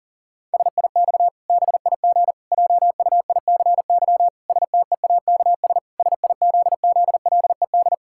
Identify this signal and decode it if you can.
Morse code, heard as a keyed tone, about 40 wpm, 700 Hz. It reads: SIX BIG JUICY STEAKS SIZZLED
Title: Pangram Challenge v2 - 40wpm